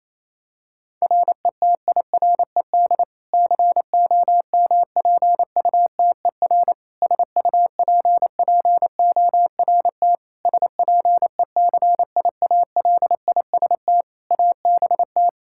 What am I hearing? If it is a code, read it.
Morse code, heard as a keyed tone, about 28 wpm, 700 Hz. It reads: RETIRED COMPUTER SUPPORT SPECIALIST ABT
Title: Practice QSO - Exchange 3 - More Info - 28wpm